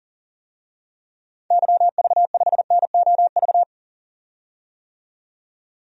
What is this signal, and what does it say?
Morse code, heard as a keyed tone, about 40 wpm, 700 Hz. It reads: YV5NOV
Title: DXCC-100 Call Signs - 6 Characters - 40wpm